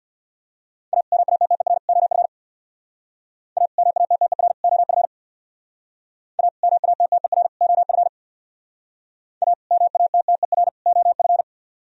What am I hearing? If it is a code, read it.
Morse code, heard as a keyed tone, about 63 wpm, 700 Hz. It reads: A MATTER OF A MATTER OF A MATTER OF A MATTER OF